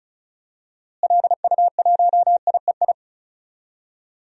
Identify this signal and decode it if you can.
Morse code, heard as a keyed tone, about 34 wpm, 700 Hz. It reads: LU1IEI
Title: DXCC-100 Call Signs - 6 Characters - Repeated 2x - 35wpm